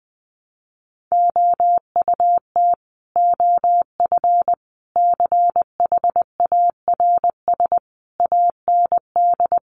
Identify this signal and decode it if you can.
Morse code, heard as a keyed tone, about 20 wpm, 700 Hz. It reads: OUT OF CHARS AND
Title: A Sound Of Thunder 20wpm